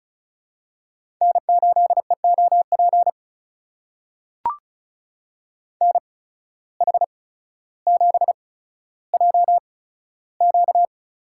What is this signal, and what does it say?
Morse code, heard as a keyed tone, about 36 wpm, spaced as for 15 wpm, 700 Hz. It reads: N8EOP NH7JQ
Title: USA Call Signs - 5 Characters - 8x Character Spacing - 35wpm